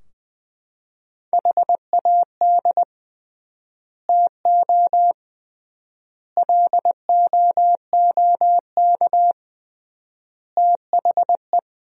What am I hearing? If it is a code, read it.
Morse code, heard as a keyed tone, about 20 wpm, 700 Hz. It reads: HAD TO LOOK THE
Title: 3x Word-Spacing - Sentences from Top 100 Words - 20wpm